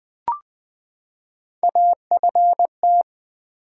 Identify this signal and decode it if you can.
Morse code, heard as a keyed tone, about 20 wpm, 700 Hz. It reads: AFT
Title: Build A Word - 3 Letters At A Time - 9-Letter Words - 20wpm